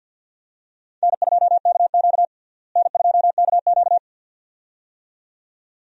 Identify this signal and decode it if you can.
Morse code, heard as a keyed tone, about 49 wpm, 700 Hz. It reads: N2KQ N2KQ
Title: USA Call Signs - 4 Characters - Repeated 2x - 50wpm